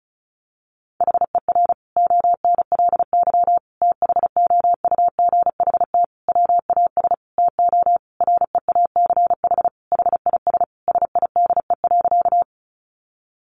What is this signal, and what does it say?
Morse code, heard as a keyed tone, about 34 wpm, 700 Hz. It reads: HER ONLY THOUGHT WAS TO REACH HIS SIDE.
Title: Sentences from Top 400 Words 35wpm